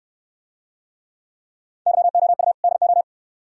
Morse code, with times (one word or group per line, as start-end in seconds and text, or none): 1.9-2.5 YOU
2.6-3.0 DO